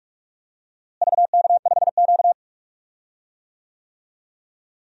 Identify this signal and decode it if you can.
Morse code, heard as a keyed tone, about 46 wpm, 700 Hz. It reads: VK5Q
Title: DXCC-100 Call Signs - 4 Characters - Repeated 2x - 45wpm